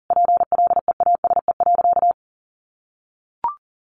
Morse code, heard as a keyed tone, about 40 wpm, 700 Hz. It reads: PLEASE.
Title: Sentences from Top 1500 Words 40wpm